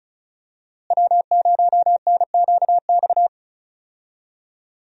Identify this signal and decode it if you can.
Morse code, heard as a keyed tone, about 34 wpm, 700 Hz. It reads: W0NQX
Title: USA Call Signs - 5 Characters - Repeated 2x - 35wpm